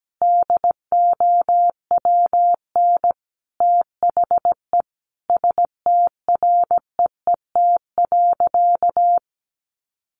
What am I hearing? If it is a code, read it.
Morse code, heard as a keyed tone, about 17 wpm, 700 Hz. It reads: DOWN THE STREET.
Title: Sentences from Top 1200 Words 17wpm